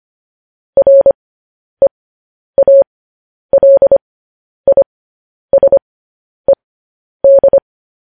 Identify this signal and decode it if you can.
Morse code, heard as a keyed tone, about 25 wpm, spaced as for 10 wpm, 550 Hz. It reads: REALISED
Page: news-headlines-in-morse-code-at-10-wpm